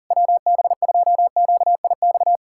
Morse code, heard as a keyed tone, about 44 wpm, 700 Hz. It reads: WB2QIX